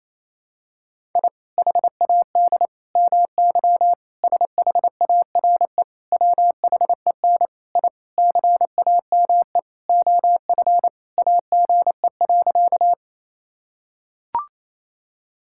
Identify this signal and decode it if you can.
Morse code, heard as a keyed tone, about 28 wpm, 700 Hz. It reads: I HAD MY SHARE WHEN I CAME OF AGE.
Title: Sentences from Top 1000 Words - Review Part 6 - 28wpm